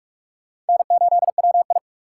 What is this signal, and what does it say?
Morse code, heard as a keyed tone, about 40 wpm, 700 Hz. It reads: N8WI